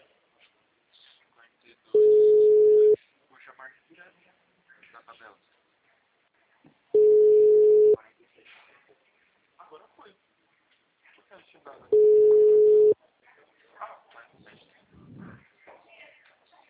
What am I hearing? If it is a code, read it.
Ringback tone (Europe, 425 Hz), with speech.